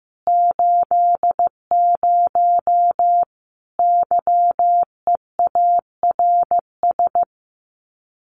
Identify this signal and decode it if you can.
Morse code, heard as a keyed tone, about 15 wpm, 700 Hz. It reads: 80 YEARS
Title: Common QSO Phrases 15wpm